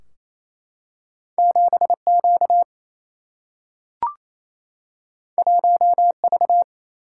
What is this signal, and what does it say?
Morse code, heard as a keyed tone, about 28 wpm, 700 Hz. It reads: 7Q 1V
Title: Single-Letter Single-Number 28wpm